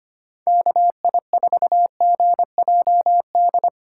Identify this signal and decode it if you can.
Morse code, heard as a keyed tone, about 25 wpm, 700 Hz. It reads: KI4GJD
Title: USA Call Signs - 6 Characters - 4x Character Spacing - 25wpm